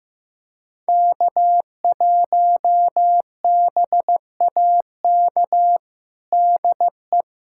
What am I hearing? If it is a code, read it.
Morse code, heard as a keyed tone, about 15 wpm, 700 Hz. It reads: K1BAK DE
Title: USA Callsign DE Callsign - 5 Characters - 15wpm